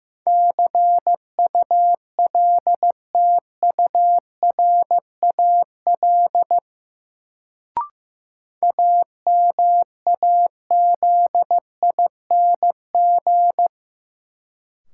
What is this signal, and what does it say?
Morse code, heard as a keyed tone, about 15 wpm, 700 Hz. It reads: CULTURAL AMAZING
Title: Sentences from Top 1500 Words - New Word Drill - 15wpm